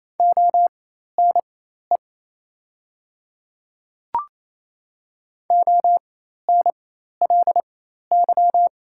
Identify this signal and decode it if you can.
Morse code, heard as a keyed tone, about 27 wpm, spaced as for 13 wpm, 690 Hz. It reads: ONE ONLY
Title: 4x Character Spacing - Top 200 Words - 28wpm